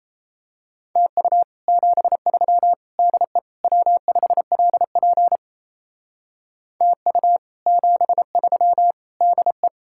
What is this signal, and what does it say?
Morse code, heard as a keyed tone, about 32 wpm, 700 Hz. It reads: TU 73 DE W5LP TU 73 DE
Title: Speed-Racing Common QSO Phrases - 28wpm